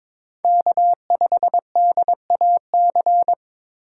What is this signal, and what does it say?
Morse code, heard as a keyed tone, about 21 wpm, 700 Hz. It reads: K5DAC
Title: USA Call Signs - 5 Characters - 2x Character Spacing - 22wpm